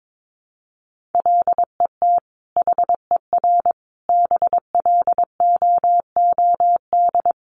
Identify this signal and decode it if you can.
Morse code, heard as a keyed tone, about 22 wpm, 700 Hz. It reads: LET HER BLOOD
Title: Sentences from Top 1200 Words 22wpm